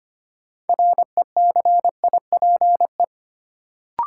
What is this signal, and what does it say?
Morse code, heard as a keyed tone, about 25 wpm, 700 Hz. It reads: RECIPE
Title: Sentences from Top 1200 Words - New Word Drill - 25wpm